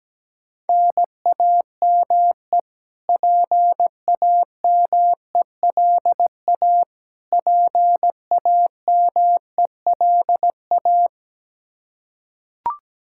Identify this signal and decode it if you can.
Morse code, heard as a keyed tone, about 17 wpm, 700 Hz. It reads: NAME PAMELA PAMELA
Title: Practice QSO - Exchange 1 - Beginning - 17wpm